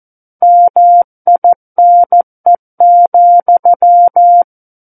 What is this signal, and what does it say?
Morse code, heard as a keyed tone, about 14 wpm, 700 Hz. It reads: MINE,
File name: Book One / 05-Chapter 5 - 14 WPM